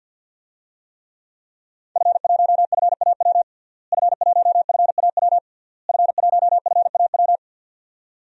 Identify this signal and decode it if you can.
Morse code, heard as a keyed tone, about 52 wpm, 700 Hz. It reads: F1FAW F1FAW F1FAW
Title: DXCC-100 Call Signs - 5 Characters - Repeated 3x - 50wpm